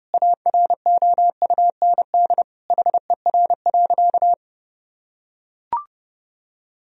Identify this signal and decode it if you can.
Morse code, heard as a keyed tone, about 32 wpm, 700 Hz. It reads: AROUND HER.
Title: Sentences from Top 1000 Words - Review Part 4 - 30wpm